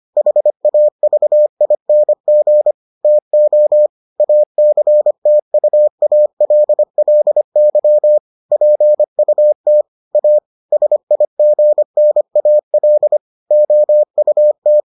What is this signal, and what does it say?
Morse code, heard as a keyed tone, about 25 wpm, 600 Hz. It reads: HAVING TO ACTUALLY PUT A SIGNAL OUT